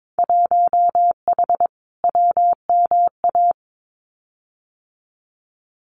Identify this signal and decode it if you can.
Morse code, heard as a keyed tone, about 22 wpm, 700 Hz. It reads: 1H WMA
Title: Winter Field Day Exchanges 22wpm